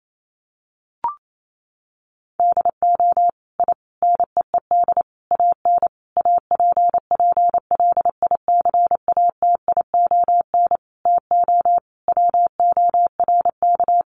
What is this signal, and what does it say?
Morse code, heard as a keyed tone, about 28 wpm, 700 Hz. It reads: DO I NEED AN APPLICATION TO WORK
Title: Sentences from Top 1200 Words 28wpm